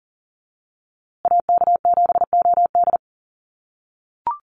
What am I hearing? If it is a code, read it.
Morse code, heard as a keyed tone, about 39 wpm, 700 Hz. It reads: AK7OD
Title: USA Call Signs Repeated 3x 40wpm